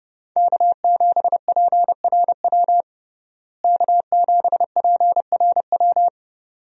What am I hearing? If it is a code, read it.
Morse code, heard as a keyed tone, about 30 wpm, 700 Hz. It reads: K7PRW K7PRW
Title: USA Call Signs - 5 Characters - Repeated 2x - 30wpm